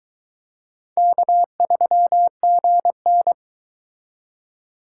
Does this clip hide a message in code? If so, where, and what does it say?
Yes: Morse code from 1.0 to 3.3 s, heard as a keyed tone, about 23 wpm, 700 Hz. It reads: K3GN